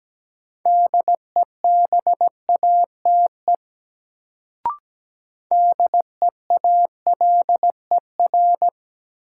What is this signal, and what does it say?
Morse code, heard as a keyed tone, about 17 wpm, 700 Hz. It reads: DEBATE DEALER
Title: Sentences from Top 1100 Words - New Word Drill Alt - 17wpm